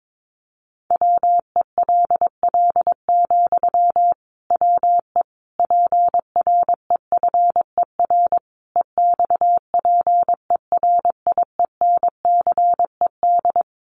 Morse code, heard as a keyed tone, about 22 wpm, 700 Hz. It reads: WELL, WE PREFER EXPERIENCED
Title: Sentences from Top 1100 Words 22wpm